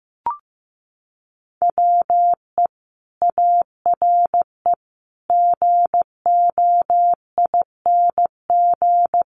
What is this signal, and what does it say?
Morse code, heard as a keyed tone, about 15 wpm, 700 Hz. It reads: WE ARE GOING
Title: Sets of 5 Words Encore 15wpm